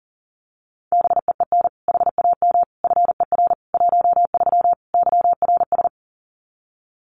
Morse code, heard as a keyed tone, about 40 wpm, 700 Hz. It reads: BEEN HAM FER 13 YRS
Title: Common QSO Phrases 40wpm